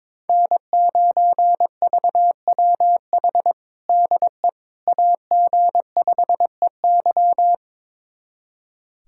Morse code, heard as a keyed tone, about 22 wpm, 700 Hz. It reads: N9VWH DE AG5EY
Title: USA Callsign DE Callsign - 5 Characters - 22wpm